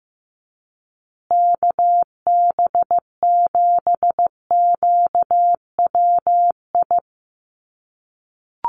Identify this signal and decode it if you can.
Morse code, heard as a keyed tone, about 15 wpm, 700 Hz. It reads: KB7QWI
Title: USA Call Signs Repeated 3x 15wpm